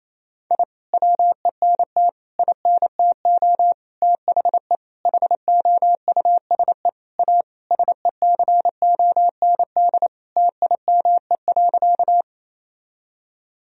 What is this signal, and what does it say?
Morse code, heard as a keyed tone, about 28 wpm, 700 Hz. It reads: I WENT INTO THE HOUSE A SECOND TIME.
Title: Sentences from Top 1000 Words - Review Part 1 - 28wpm